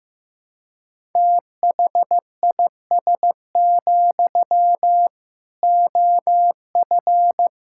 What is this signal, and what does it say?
Morse code, heard as a keyed tone, about 15 wpm, 700 Hz. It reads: THIS, OF
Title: Sentences from Top 1600 Words 15wpm